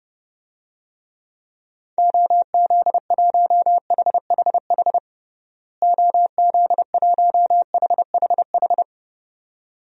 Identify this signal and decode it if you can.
Morse code, heard as a keyed tone, about 32 wpm, 700 Hz. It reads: OZ1HHH OZ1HHH
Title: DXCC-100 Call Signs - 6 Characters - Repeated 2x - 30wpm